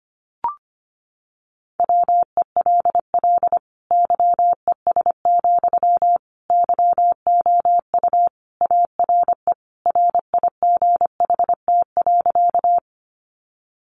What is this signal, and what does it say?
Morse code, heard as a keyed tone, about 25 wpm, 700 Hz. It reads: WELL YES, YOU ARE RIGHT.
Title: Sentences from Top 500 Words 25wpm